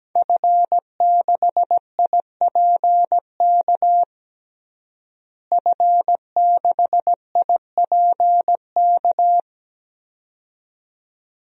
Morse code, heard as a keyed tone, about 17 wpm, 700 Hz. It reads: F6IPK F6IPK
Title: DXCC-100 Call Signs - 5 Characters - Repeated 2x - 17wpm